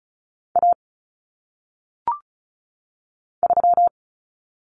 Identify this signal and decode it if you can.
Morse code, heard as a keyed tone, about 36 wpm, 700 Hz. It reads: A 3
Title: Common Cut Numbers - 1-Digit 35wpm